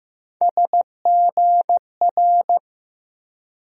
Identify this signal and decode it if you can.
Morse code, heard as a keyed tone, about 15 wpm, 700 Hz. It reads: SGR
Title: USA Call Sign Postfixes - 3 Letters 15wpm